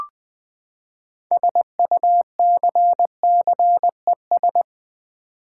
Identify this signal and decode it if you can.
Morse code, heard as a keyed tone, about 20 wpm, 700 Hz. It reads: SUCCES
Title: Build A Word - 6 Letters At A Time - 12-Letter Words - 20wpm